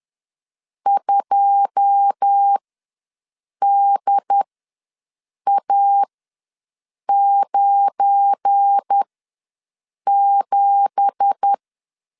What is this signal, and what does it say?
Morse code, heard as a keyed tone, about 10 wpm, spaced as for 6 wpm, 800 Hz. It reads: 2DA97
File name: LEZIONI / 10 SIGNS / 10D COMA ws 10wpm 4314